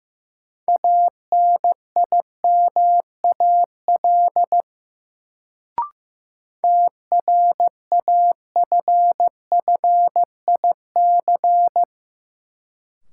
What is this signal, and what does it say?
Morse code, heard as a keyed tone, about 15 wpm, 700 Hz. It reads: ANIMAL TRAFFIC
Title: Sentences from Top 1300 Words - New Word Drill Alt - 15wpm